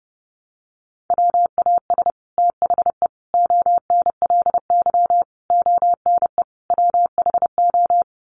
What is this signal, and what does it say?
Morse code, heard as a keyed tone, about 30 wpm, 700 Hz. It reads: WAS THE ONLY ONE WHO
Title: Sets of 5 Words Encore2 30wpm